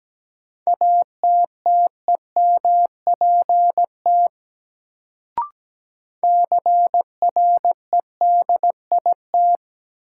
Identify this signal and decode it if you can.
Morse code, heard as a keyed tone, about 17 wpm, 700 Hz. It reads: ATTEMPT CREDIT
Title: Sentences from Top 1000 Words - New Word Drill - 17wpm